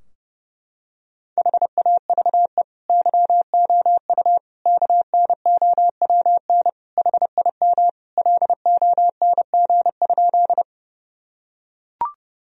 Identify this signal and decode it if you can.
Morse code, heard as a keyed tone, about 30 wpm, 700 Hz. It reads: HAVE YOU KNOWN HIM LONG?
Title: Sentences from Top 1000 Words - Review Part 3 - 30wpm